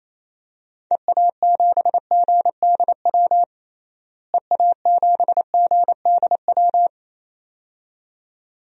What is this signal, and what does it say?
Morse code, heard as a keyed tone, about 28 wpm, 700 Hz. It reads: EA7GDW EA7GDW
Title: DXCC-100 Call Signs - 6 Characters - Repeated 2x - 28wpm